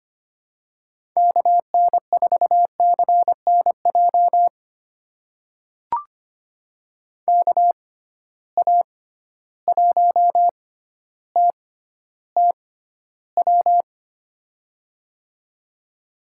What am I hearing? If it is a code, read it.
Morse code, heard as a keyed tone, about 25 wpm, spaced as for 13 wpm, 700 Hz. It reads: KN4CNJ KA1TTW
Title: USA Call Signs - 6 Characters - 6x Character Spacing - 25wpm